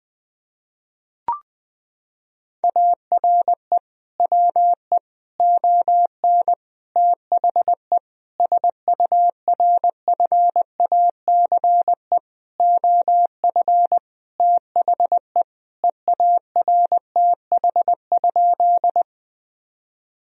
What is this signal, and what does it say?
Morse code, heard as a keyed tone, about 20 wpm, 700 Hz. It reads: ARE WE ON THE SURFACE OF THE EARTH?
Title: Sentences from Top 1000 Words - Review Part 3 - 20wpm